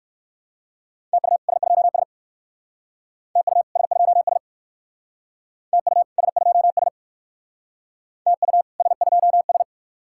Morse code, heard as a keyed tone, about 58 wpm, 700 Hz. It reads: TU S2S TU S2S TU S2S TU S2S